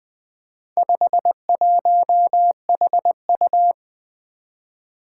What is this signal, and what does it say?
Morse code, heard as a keyed tone, about 20 wpm, 700 Hz. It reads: 51HU
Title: Four Challenging Letter-Number Combinations 20wpm